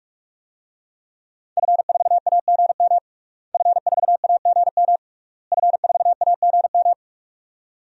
Morse code, heard as a keyed tone, about 45 wpm, 700 Hz. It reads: F4AGM F4AGM F4AGM
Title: DXCC-100 Call Signs - 5 Characters - Repeated 3x - 45wpm